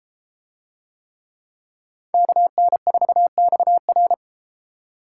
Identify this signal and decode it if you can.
Morse code, heard as a keyed tone, about 32 wpm, 700 Hz. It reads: KN4XR